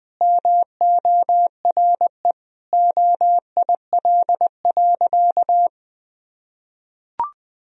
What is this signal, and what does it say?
Morse code, heard as a keyed tone, about 20 wpm, 700 Hz. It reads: MORE OIL.
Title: Sentences from Top 1000 Words - Review Part 4 - 20wpm